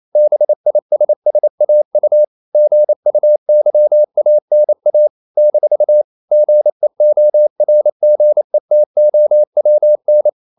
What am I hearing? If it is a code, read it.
Morse code, heard as a keyed tone, about 28 wpm, 600 Hz. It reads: BISSAU GUYANA - GEORGETOWN
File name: CW / Practice / NationalCapitals28